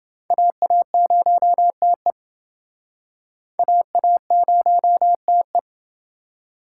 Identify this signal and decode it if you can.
Morse code, heard as a keyed tone, about 29 wpm, 700 Hz. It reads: AA0TE AA0TE